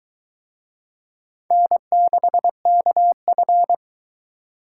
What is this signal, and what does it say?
Morse code, heard as a keyed tone, about 20 wpm, 700 Hz. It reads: N6KF